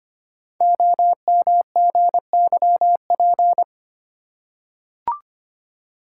Morse code, heard as a keyed tone, about 25 wpm, 700 Hz. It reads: OMGYP
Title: Code Groups - 5 Letters - 25wpm